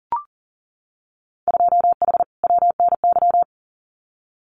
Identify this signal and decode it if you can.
Morse code, heard as a keyed tone, about 40 wpm, 700 Hz. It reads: 2H WNY
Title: Winter Field Day Exchanges Encore 40wpm